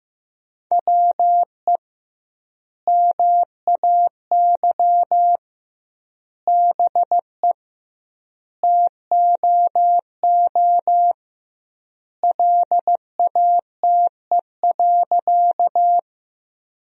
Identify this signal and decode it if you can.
Morse code, heard as a keyed tone, about 15 wpm, 700 Hz. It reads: WE MAY BE TOO LATE.
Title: 2x Word-Spacing - Sentences from Top 300 Words - 15wpm